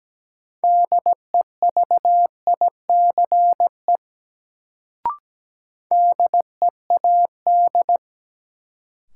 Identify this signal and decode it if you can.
Morse code, heard as a keyed tone, about 17 wpm, 700 Hz. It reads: DEVICE DEAD
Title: Sentences from Top 1500 Words - New Word Drill Alt - 17wpm